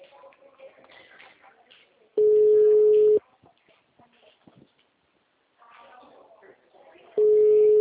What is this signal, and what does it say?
Ringback tone (Europe, 425 Hz), with speech.